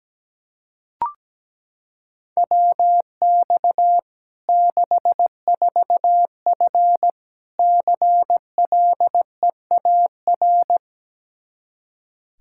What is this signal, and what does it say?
Morse code, heard as a keyed tone, about 17 wpm, 700 Hz. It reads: WX 64F CLEAR
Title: Practice QSO - Exchange 2 - Info - 17wpm